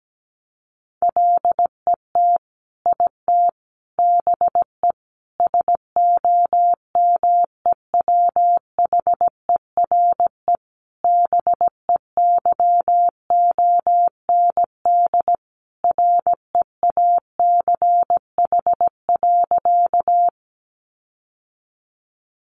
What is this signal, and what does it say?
Morse code, heard as a keyed tone, about 17 wpm, 700 Hz. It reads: LET IT BE SOMEWHERE BEYOND REACH.
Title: Sentences from Top 1400 Words 17wpm